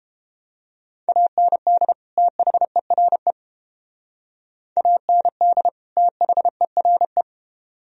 Morse code, heard as a keyed tone, about 31 wpm, 700 Hz. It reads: AND THERE AND THERE